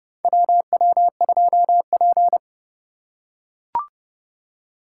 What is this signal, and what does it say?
Morse code, heard as a keyed tone, about 30 wpm, 700 Hz. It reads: WW2P
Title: K1USN SST QSO Practice - 30wpm